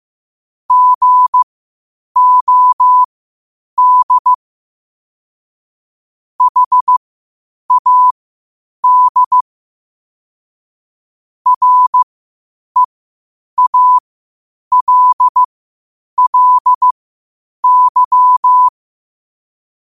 Morse code, heard as a keyed tone, about 15 wpm, spaced as for 8 wpm, 1000 Hz. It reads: GOD HAD REALLY
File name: qotdfiles / 8wpm